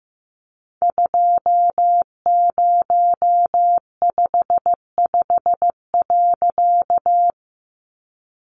Morse code, heard as a keyed tone, about 15 wpm, 700 Hz. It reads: 2055.
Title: A Sound Of Thunder 15wpm